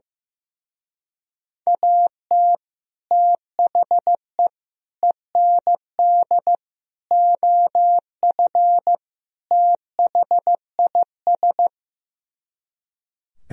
Morse code, heard as a keyed tone, about 15 wpm, 700 Hz. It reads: AT THE END OF THIS
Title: Speed-Racing - Sets of 5 Words 15wpm